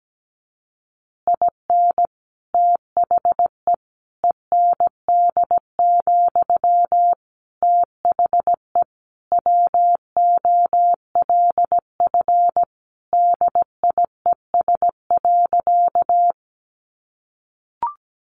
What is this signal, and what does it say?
Morse code, heard as a keyed tone, about 17 wpm, 700 Hz. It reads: IN THE END, THE WOLF DIES.
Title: Additional Easy Phrases 17wpm